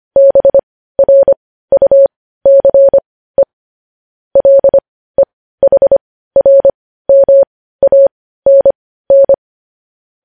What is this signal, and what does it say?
Morse code, heard as a keyed tone, about 25 wpm, spaced as for 15 wpm, 550 Hz. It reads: BRUCE LEHRMANN